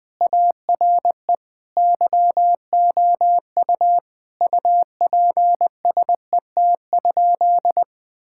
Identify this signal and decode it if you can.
Morse code, heard as a keyed tone, about 20 wpm, 700 Hz. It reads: ARE YOU UPSET?
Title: Sentences from Top 1000 Words - Review Part 5 - 20wpm